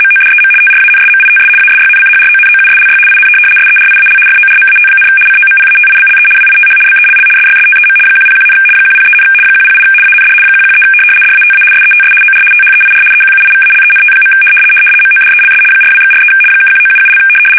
RTTY, about 75 baud. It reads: MODEM TRANSMITS ASYNCHRONOUSLY IN SERIAL FORMAT, COMPATIBLE WITH BELL SYSTEM 101, 103, 108 AND 113 DATA SETS OR THEIR EQUIVALENT. MODELS 42/43 (1977)
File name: fsk850-75B / teletype-corp